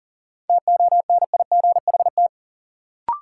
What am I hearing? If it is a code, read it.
Morse code, heard as a keyed tone, about 40 wpm, 700 Hz. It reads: TONIGHT
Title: Sentences from Top 600 Words - New Word Drill - 40wpm